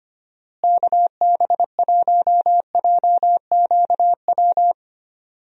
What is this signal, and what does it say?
Morse code, heard as a keyed tone, about 25 wpm, 700 Hz. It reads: KB1JQW
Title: USA Call Signs - 6 Characters - Repeated 3x - 25wpm